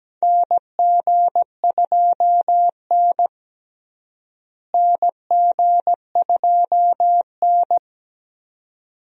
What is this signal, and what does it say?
Morse code, heard as a keyed tone, about 17 wpm, 700 Hz. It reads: NG2N NG2N
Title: USA Call Signs - 4 Characters - Repeated 2x - 17wpm